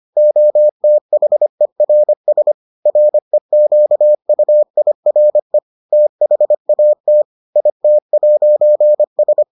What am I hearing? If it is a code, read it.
Morse code, heard as a keyed tone, about 25 wpm, 600 Hz. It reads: OTHERS REQUIRE THAT IT'S